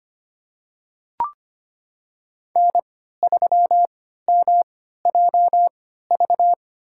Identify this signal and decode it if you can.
Morse code, heard as a keyed tone, about 25 wpm, spaced as for 14 wpm, 700 Hz. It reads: N3MJV
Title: USA Call Signs - 5 Characters - 3x Character Spacing - 25wpm